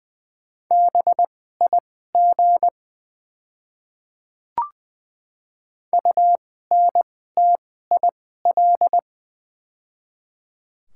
Morse code, heard as a keyed tone, about 20 wpm, spaced as for 14 wpm, 700 Hz. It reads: BIG UNTIL
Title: 2x Character Spacing - Top 500 Words - 20wpm